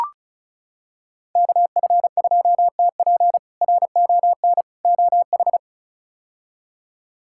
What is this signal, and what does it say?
Morse code, heard as a keyed tone, about 36 wpm, 700 Hz. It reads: KF2TP RON OH
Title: K1USN SST QSO Practice - 35wpm